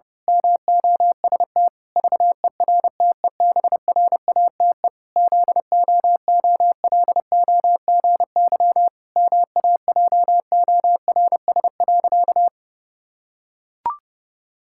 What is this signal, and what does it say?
Morse code, heard as a keyed tone, about 30 wpm, 700 Hz. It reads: MOST VERTEBRATE ZOOLOGY MAJORS.
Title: Pangram Challenge v2 - 30wpm